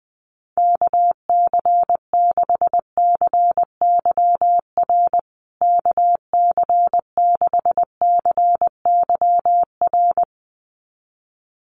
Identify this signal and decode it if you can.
Morse code, heard as a keyed tone, about 19 wpm, 700 Hz. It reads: KC6CYR KC6CYR
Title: USA Call Signs Repeated 2x 20wpm